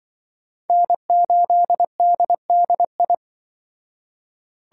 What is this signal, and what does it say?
Morse code, heard as a keyed tone, about 25 wpm, 700 Hz. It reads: N8DDI
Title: Speed Racing - USA Call Signs 20wpm